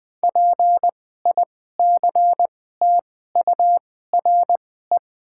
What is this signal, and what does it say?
Morse code, heard as a keyed tone, about 20 wpm, spaced as for 14 wpm, 700 Hz. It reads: PICTURE
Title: 2x Character Spacing - Top 300 Words - 20wpm